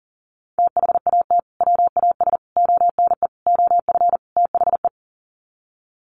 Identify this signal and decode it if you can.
Morse code, heard as a keyed tone, about 40 wpm, 700 Hz. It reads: THAT WAS ONE OF THE
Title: Sets of 5 Words 40wpm